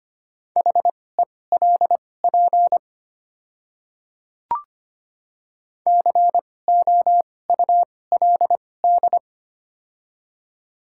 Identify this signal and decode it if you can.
Morse code, heard as a keyed tone, about 25 wpm, spaced as for 18 wpm, 700 Hz. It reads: HELP COULD
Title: 2x Character Spacing - Top 200 Words - 25wpm